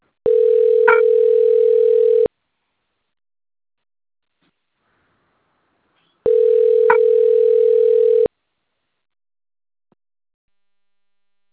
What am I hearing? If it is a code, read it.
Ringback tone (North America), 440 + 480 Hz.
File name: out- - - - - .99261